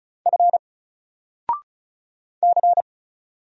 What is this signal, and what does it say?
Morse code, heard as a keyed tone, about 36 wpm, 700 Hz. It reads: F C
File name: Single-Letter-Number-35wpm